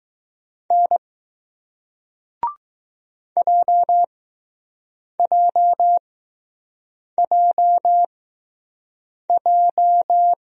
Morse code, heard as a keyed tone, about 20 wpm, spaced as for 6 wpm, 700 Hz. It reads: N JJJJ